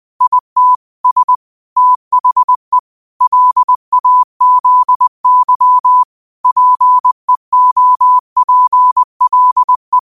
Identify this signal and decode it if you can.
Morse code, heard as a keyed tone, about 20 wpm, 1000 Hz. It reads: IT S THE LAZY PEOPLE
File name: qotdfiles / 20wpm